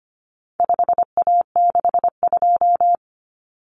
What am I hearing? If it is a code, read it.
Morse code, heard as a keyed tone, about 25 wpm, 700 Hz. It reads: 5A62
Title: Common Cut Numbers - 4-Digit 25wpm